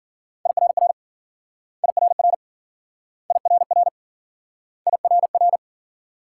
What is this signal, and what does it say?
Morse code, heard as a keyed tone, about 52 wpm, 700 Hz. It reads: IRR IRR IRR IRR